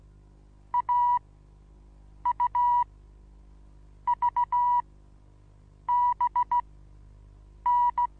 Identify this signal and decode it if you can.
Morse code, heard as a keyed tone, about 16 wpm, spaced as for 6 wpm, 970 Hz. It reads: AUVBN
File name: Onlinelehrgang / morsen / DJ4UF Morsen25